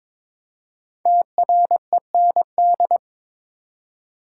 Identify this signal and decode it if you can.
Morse code, heard as a keyed tone, about 22 wpm, 700 Hz. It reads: TREND
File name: 5 Letter Words - First 3 Letters Given - 22wpm